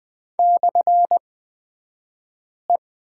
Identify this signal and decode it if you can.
Morse code, heard as a keyed tone, about 20 wpm, spaced as for 5 wpm, 700 Hz. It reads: /E
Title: 223 Lesson 36d - BT - Two Characters 20wpm